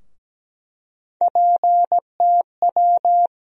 Morse code, heard as a keyed tone, about 17 wpm, 700 Hz. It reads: PTW
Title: USA Call Sign Postfixes - 3 Letters 17wpm